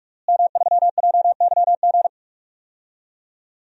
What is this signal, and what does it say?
Morse code, heard as a keyed tone, about 46 wpm, 700 Hz. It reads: M3JYG
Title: DXCC-100 Call Signs - 5 Characters - Repeated 3x - 45wpm